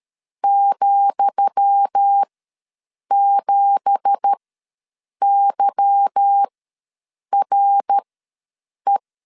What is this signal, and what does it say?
Morse code, heard as a keyed tone, about 13 wpm, spaced as for 7 wpm, 800 Hz. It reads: ,7YRE
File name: LEZIONI / 10 SIGNS / 10D COMA ws 12wpm 3618